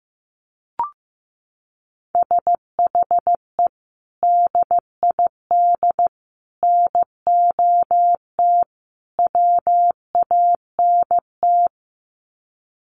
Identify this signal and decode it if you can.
Morse code, heard as a keyed tone, about 15 wpm, 700 Hz. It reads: SHE DID NOT WANT
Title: Sets of 4 Words 15wpm